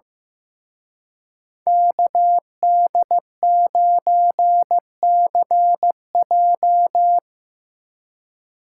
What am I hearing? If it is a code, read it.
Morse code, heard as a keyed tone, about 17 wpm, 700 Hz. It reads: KD9CJ
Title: Speed Racing - USA Call Signs 15wpm